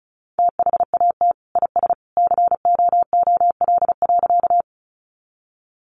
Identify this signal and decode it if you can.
Morse code, heard as a keyed tone, about 34 wpm, 700 Hz. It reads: THAT IS COOL.
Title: Sentences from Top 1200 Words 35wpm